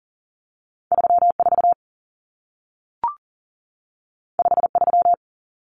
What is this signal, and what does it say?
Morse code, heard as a keyed tone, about 40 wpm, 700 Hz. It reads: 34 53
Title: Numbers - 2-Digit 40wpm